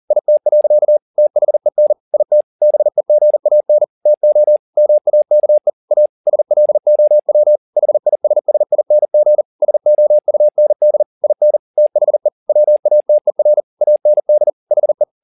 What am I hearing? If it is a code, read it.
Morse code, heard as a keyed tone, about 40 wpm, 600 Hz. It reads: IT. THEN IT BEGAN TO MAKE A SLOW HISSING SOUND IN THE WATER AND HE